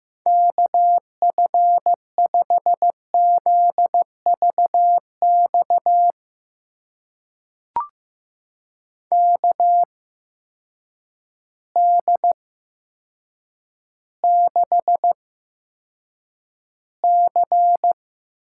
Morse code, heard as a keyed tone, about 15 wpm, spaced as for 7 wpm, 700 Hz. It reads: KF5ZVX KD6C
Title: USA Call Signs - 6 Characters - 8x Character Spacing - 15wpm